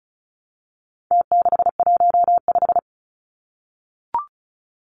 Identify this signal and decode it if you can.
Morse code, heard as a keyed tone, about 34 wpm, 700 Hz. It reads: T615
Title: Numbers - Contest Style - 4-Digit 35wpm